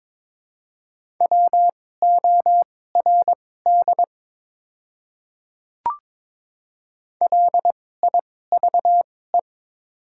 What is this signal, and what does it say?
Morse code, heard as a keyed tone, about 22 wpm, spaced as for 16 wpm, 700 Hz. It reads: WORD LIVE
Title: 2x Character Spacing - Top 500 Words - 22wpm